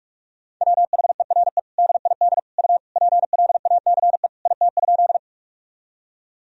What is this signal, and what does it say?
Morse code, heard as a keyed tone, about 46 wpm, 700 Hz. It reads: WHERE DID U PLACE IT?
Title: Sentences from Top 200 Words and Abbreviations 45wpm